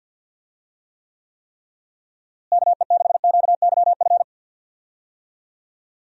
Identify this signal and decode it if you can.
Morse code, heard as a keyed tone, about 50 wpm, 700 Hz. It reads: KE6QYF
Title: USA Call Signs - 6 Characters - Repeated 3x - 50wpm